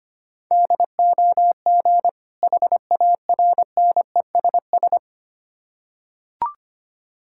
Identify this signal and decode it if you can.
Morse code, heard as a keyed tone, about 25 wpm, 700 Hz. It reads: DOG HARNESS
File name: practice / 25wpm / Difficult Words with Hints - 25wpm